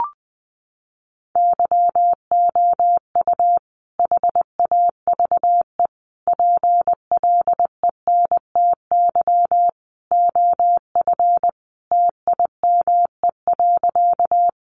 Morse code, heard as a keyed tone, about 20 wpm, 700 Hz. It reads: YOU HAVE PLENTY OF TIME.
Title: Sentences from Top 1300 Words 20wpm